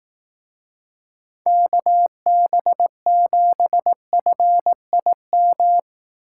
Morse code, heard as a keyed tone, about 18 wpm, 700 Hz. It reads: KB7FIM
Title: Speed Racing - USA 2x3 Call Signs 15wpm